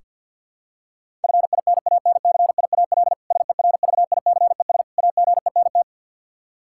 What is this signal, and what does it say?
Morse code, heard as a keyed tone, about 48 wpm, 700 Hz. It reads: FINANCIAL SERVICES AGENT
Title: K1USN SST QSO Practice 50wpm